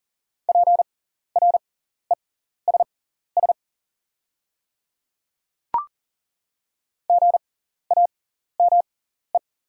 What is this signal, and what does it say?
Morse code, heard as a keyed tone, about 40 wpm, spaced as for 14 wpm, 700 Hz. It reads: PRESS GAME
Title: 6x Character Spacing - Top 500 Words - 40wpm